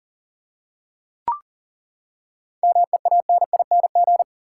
Morse code, heard as a keyed tone, about 40 wpm, 700 Hz. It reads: MEANING
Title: Seven Letter Words - 40wpm